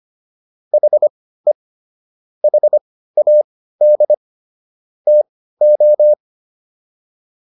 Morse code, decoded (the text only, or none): HE HAD TO